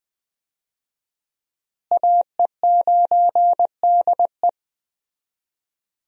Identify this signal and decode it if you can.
Morse code, heard as a keyed tone, about 21 wpm, 700 Hz. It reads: AE9DE
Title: USA Call Signs - 5 Characters - Repeated 2x - 20wpm